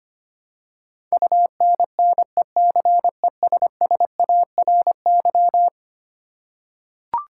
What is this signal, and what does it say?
Morse code, heard as a keyed tone, about 25 wpm, 700 Hz. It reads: UNNECESSARY
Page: morse-code-ninja-practice-25wpm